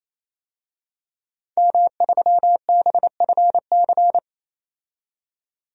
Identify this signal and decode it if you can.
Morse code, heard as a keyed tone, about 27 wpm, 700 Hz. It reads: M3BFC
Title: DXCC-100 Call Signs - 5 Characters - 28wpm